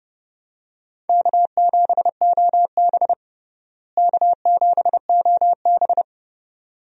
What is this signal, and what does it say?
Morse code, heard as a keyed tone, about 30 wpm, 700 Hz. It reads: K7OB K7OB
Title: USA Call Signs - 4 Characters - Repeated 2x - 30wpm